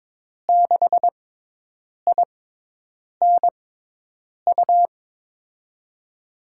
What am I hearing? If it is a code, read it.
Morse code, heard as a keyed tone, about 21 wpm, spaced as for 8 wpm, 700 Hz. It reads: 6INU
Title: USA Call Signs - 6 Characters - 6x Character Spacing - 22wpm